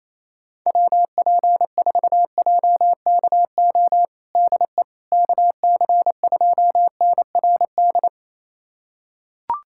Morse code, heard as a keyed tone, about 27 wpm, 700 Hz. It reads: WP4JKO DE KC2NRD
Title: USA Callsign DE Callsign 28wpm